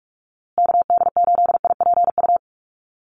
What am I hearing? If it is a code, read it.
Morse code, heard as a keyed tone, about 44 wpm, 700 Hz. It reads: KD8IPU
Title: USA Call Signs Repeated 2x 45wpm